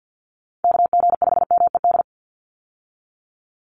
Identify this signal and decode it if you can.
Morse code, heard as a keyed tone, about 50 wpm, 700 Hz. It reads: KG5MED
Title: USA Call Signs 50wpm